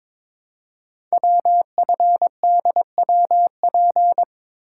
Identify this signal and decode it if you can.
Morse code, heard as a keyed tone, about 22 wpm, 700 Hz. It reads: WFDWP
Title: Code Groups - 5 Letters - 22wpm